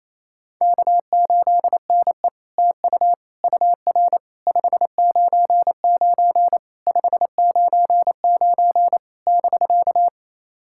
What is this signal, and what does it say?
Morse code, heard as a keyed tone, about 28 wpm, 700 Hz. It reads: K8NE TU UR 599 599 <BK>
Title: SOTA QSO Practice 28wpm